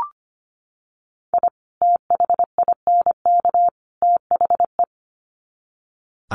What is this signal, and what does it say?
Morse code, heard as a keyed tone, about 25 wpm, 700 Hz. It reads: I THINK THE
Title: Sets of 3 Words 25wpm